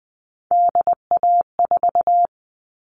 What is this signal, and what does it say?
Morse code, heard as a keyed tone, about 20 wpm, 700 Hz. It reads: DA4
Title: All Cut Numbers - 3-Digit 20wpm